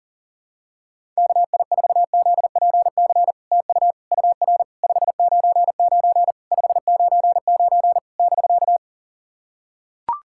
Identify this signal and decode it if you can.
Morse code, heard as a keyed tone, about 44 wpm, 700 Hz. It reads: KI4ZPC TU UR 599 599 <BK>
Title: SOTA QSO Practice 40wpm